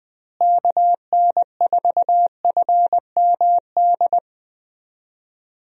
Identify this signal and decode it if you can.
Morse code, heard as a keyed tone, about 20 wpm, 700 Hz. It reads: KN4FMD
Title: USA Call Signs - 6 Characters - Repeated 3x - 20wpm